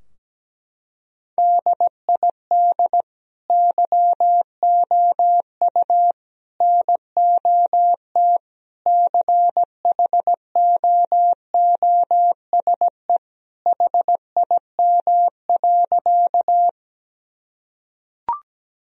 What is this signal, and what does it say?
Morse code, heard as a keyed tone, about 17 wpm, 700 Hz. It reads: DID YOU NOT CHOOSE HIM.
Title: Sentences from Top 1000 Words - Review Part 4 - 17wpm